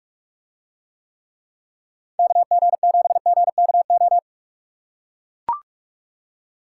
Morse code, heard as a keyed tone, about 44 wpm, 700 Hz. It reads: KG7GKO